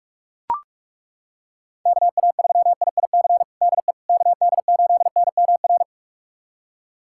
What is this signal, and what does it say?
Morse code, heard as a keyed tone, about 41 wpm, 700 Hz. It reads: KA3IIC DE KD8NMR
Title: USA Callsign DE Callsign - 6 Characters - 45wpm